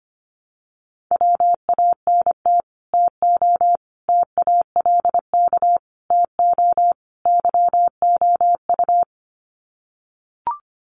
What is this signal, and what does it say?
Morse code, heard as a keyed tone, about 25 wpm, 700 Hz. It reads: WANT TO TALK TO YOU
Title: Sets of 5 Words Encore 25wpm